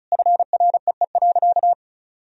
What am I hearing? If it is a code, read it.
Morse code, heard as a keyed tone, about 36 wpm, 700 Hz. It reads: FREE.
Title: Sentences from Top 1000 Words - Review Part 1 - 35wpm